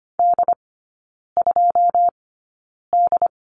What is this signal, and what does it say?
Morse code, heard as a keyed tone, about 25 wpm, spaced as for 9 wpm, 690 Hz. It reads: D2D
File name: Lesson 14e - Number 2 - Three Characters 25wpm